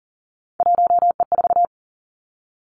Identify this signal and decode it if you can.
Morse code, heard as a keyed tone, about 40 wpm, 700 Hz. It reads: 1E4
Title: Three Challenging Letter-Number Combinations 40wpm